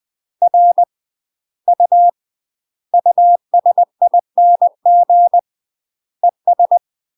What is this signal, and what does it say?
Morse code, heard as a keyed tone, about 20 wpm, 700 Hz. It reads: R U USING ES
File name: UK MASTER QSOs TRAINING / 20WPM / UK MASTER QSOs 0056